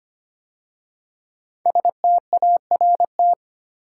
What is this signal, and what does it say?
Morse code, heard as a keyed tone, about 25 wpm, 700 Hz. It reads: START
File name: 5 Letter Words - First 3 Letters Given - 25wpm